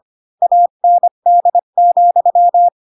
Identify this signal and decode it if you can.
Morse code, heard as a keyed tone, about 25 wpm, 700 Hz. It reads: AND,
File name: Pinocchio / Pinocchio ch8 20 25 ik1hge